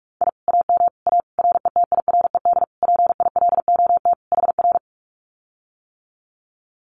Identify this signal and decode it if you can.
Morse code, heard as a keyed tone, about 46 wpm, 700 Hz. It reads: I AM A RETIRED PILOT HR
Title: Common QSO Phrases Encore 45wpm